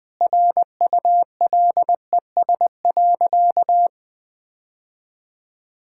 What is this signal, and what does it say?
Morse code, heard as a keyed tone, about 20 wpm, 700 Hz. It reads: RULES.
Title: Sentences with 2-Word Phrases - 20wpm